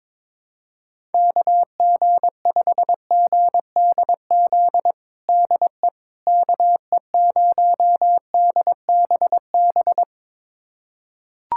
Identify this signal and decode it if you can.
Morse code, heard as a keyed tone, about 22 wpm, 700 Hz. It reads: KG5GDZ DE KE0DBB
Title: USA Callsign DE Callsign - 6 Characters - 22wpm